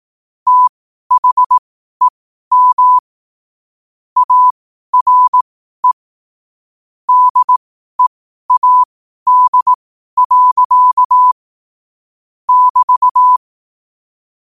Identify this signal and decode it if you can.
Morse code, heard as a keyed tone, about 18 wpm, spaced as for 12 wpm, 1000 Hz. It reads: THEM ARE DEAD. <BT>
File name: qotdfiles / 12wpm